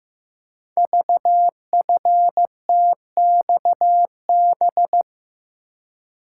Morse code, heard as a keyed tone, about 15 wpm, 700 Hz. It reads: VFTXB
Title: Code Groups - 5 Letters - 15wpm